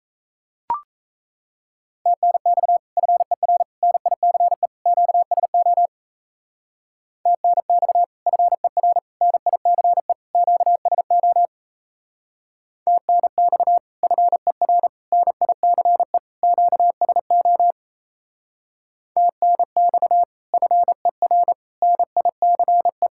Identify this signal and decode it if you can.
Morse code, heard as a keyed tone, about 34 wpm, 700 Hz. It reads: TNX FER NICE QSO TNX FER NICE QSO TNX FER NICE QSO TNX FER NICE
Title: Speed-Racing Common QSO Phrases Encore - 28wpm